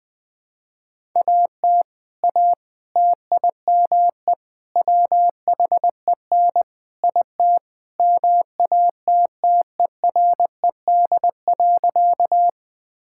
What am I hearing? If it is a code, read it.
Morse code, heard as a keyed tone, about 20 wpm, 700 Hz. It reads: AT A TIME WHEN IT MATTERED.
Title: Sentences with 4-Word Phrases - 20wpm